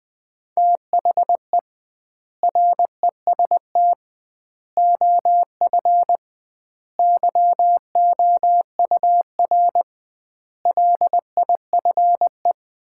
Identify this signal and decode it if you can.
Morse code, heard as a keyed tone, about 21 wpm, 700 Hz. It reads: THE REST OF YOUR LIFE
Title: 2x Word-Spacing - Sets of 5 Words - 20wpm